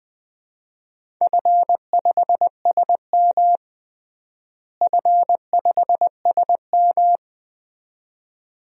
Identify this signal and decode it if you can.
Morse code, heard as a keyed tone, about 20 wpm, 700 Hz. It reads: F5SM F5SM
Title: DXCC-100 Call Signs - 4 Characters - Repeated 2x - 20wpm